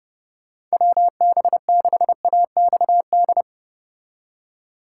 Morse code, heard as a keyed tone, about 32 wpm, 700 Hz. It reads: WB6AXD